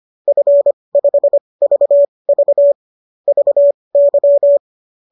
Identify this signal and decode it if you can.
Morse code, heard as a keyed tone, about 25 wpm, spaced as for 20 wpm, 590 Hz. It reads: F5VV VY
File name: qso214 20